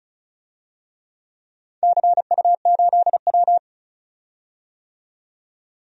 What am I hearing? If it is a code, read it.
Morse code, heard as a keyed tone, about 36 wpm, 700 Hz. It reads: CU8W
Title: DXCC-100 Call Signs - 4 Characters - 35wpm